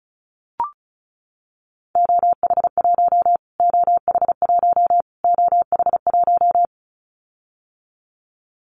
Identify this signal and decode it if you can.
Morse code, heard as a keyed tone, about 34 wpm, 700 Hz. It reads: OH1 OH1 OH1
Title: Callsign Prefixes for Top 25 DXCC - Repeated 3x - 35wpm